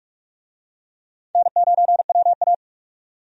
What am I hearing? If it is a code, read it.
Morse code, heard as a keyed tone, about 46 wpm, 700 Hz. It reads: N9WA
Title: USA Call Signs - 4 Characters - 45wpm